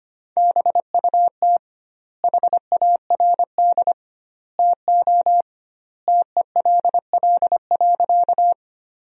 Morse code, heard as a keyed tone, about 25 wpm, 700 Hz. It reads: BUT HARD TO TELL.
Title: 2x Word-Spacing - Sentences from Top 300 Words - 25wpm